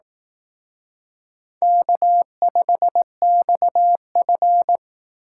Morse code, heard as a keyed tone, about 18 wpm, 700 Hz. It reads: K5XF
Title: Speed Racing - USA 1x2 Call Signs 15wpm